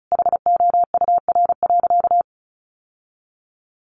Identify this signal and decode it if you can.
Morse code, heard as a keyed tone, about 34 wpm, 700 Hz. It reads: HOUR.
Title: Sentences from Top 1300 Words 35wpm